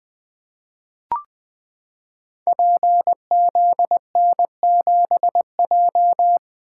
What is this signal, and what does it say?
Morse code, heard as a keyed tone, about 20 wpm, 700 Hz. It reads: PZN7J
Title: Code Groups - 5 Letter:Numbers - 20wpm